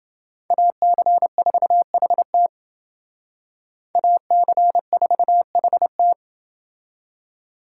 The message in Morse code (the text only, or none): AC4HT AC4HT